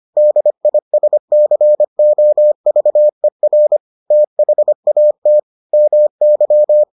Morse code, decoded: DISCOVER THAT MY

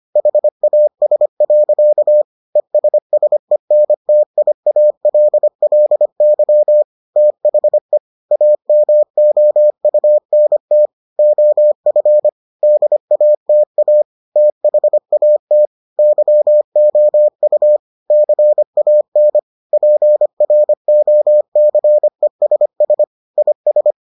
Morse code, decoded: HAS. ESSENTIALLY THE AMOUNT OF DATA THAT YOU CAN PROCESS IS

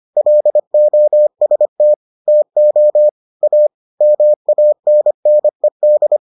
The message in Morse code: LOST TO A MANNED

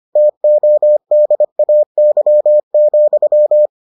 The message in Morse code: TODAY,

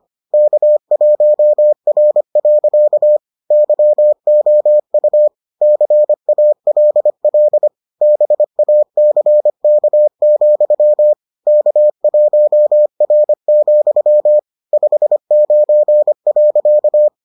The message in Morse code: K1R. YOU CALL BACK, K1R, 59.

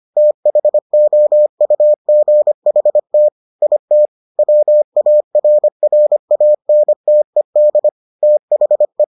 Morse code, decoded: THOUGHT IT WARRANTED THE

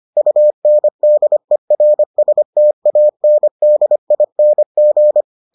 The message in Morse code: UNDERSTANDING

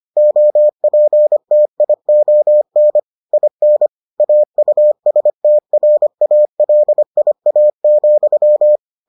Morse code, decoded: OPTION IN AUSTRALIA,

